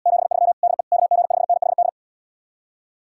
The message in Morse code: 73 DE KG5NHD

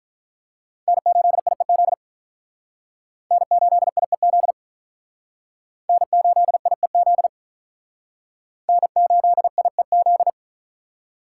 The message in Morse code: N8IEZ N8IEZ N8IEZ N8IEZ

